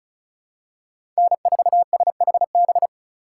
Morse code, heard as a keyed tone, about 34 wpm, 700 Hz. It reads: N4SHB
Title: USA Call Signs - 5 Characters - 2x Character Spacing - 35wpm